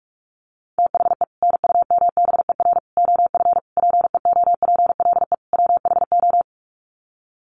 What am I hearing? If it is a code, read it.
Morse code, heard as a keyed tone, about 44 wpm, 700 Hz. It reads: THE NUMBER OF PEOPLE WHO
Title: Sets of 5 Words Encore 45wpm